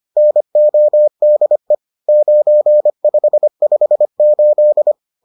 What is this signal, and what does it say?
Morse code, heard as a keyed tone, about 25 wpm, 600 Hz. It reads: NODE 9558